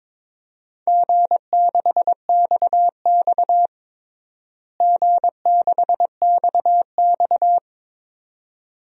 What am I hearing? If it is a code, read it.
Morse code, heard as a keyed tone, about 22 wpm, 700 Hz. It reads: G6XX G6XX
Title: DXCC-100 Call Signs - 4 Characters - Repeated 2x - 22wpm